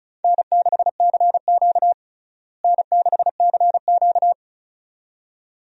Morse code, heard as a keyed tone, about 36 wpm, 700 Hz. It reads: N6CQ N6CQ
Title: USA Call Signs - 4 Characters - Repeated 2x - 35wpm